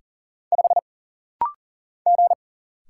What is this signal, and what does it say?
Morse code, heard as a keyed tone, about 40 wpm, 700 Hz. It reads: 5 G
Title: Single Letter-Number - Rapid-Fire 40wpm